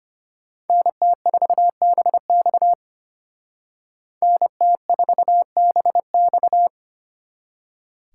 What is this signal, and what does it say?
Morse code, heard as a keyed tone, about 32 wpm, 700 Hz. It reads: NT4BX NT4BX